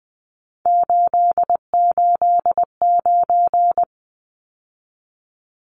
Morse code, heard as a keyed tone, about 20 wpm, 700 Hz. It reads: 889
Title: Numbers - 3-Digit 20wpm